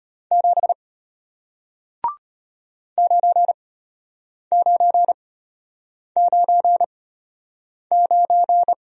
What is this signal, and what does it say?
Morse code, heard as a keyed tone, about 34 wpm, spaced as for 8 wpm, 700 Hz. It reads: 7 9999